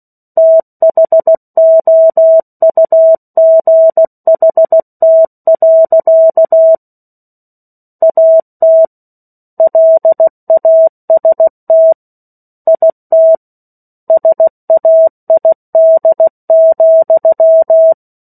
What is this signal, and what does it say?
Morse code, heard as a keyed tone, about 16 wpm, 650 Hz. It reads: THOUGHT. AT LAST IT SAID,
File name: 03-Chapter-03 16 WPM